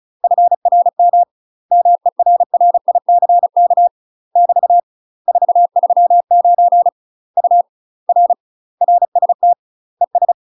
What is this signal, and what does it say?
Morse code, heard as a keyed tone, about 35 wpm, 700 Hz. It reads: FRM MERRICK <BT> 439 U R RST ES
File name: UK MASTER QSOs TRAINING / 35WPM / UK MASTER QSOs 0015